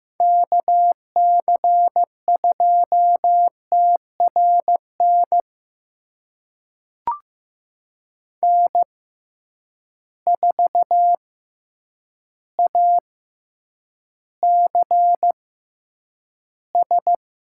Morse code, heard as a keyed tone, about 15 wpm, spaced as for 8 wpm, 700 Hz. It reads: KC2TRN N4ACS
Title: USA Call Signs - 6x Character Spacing - 15wpm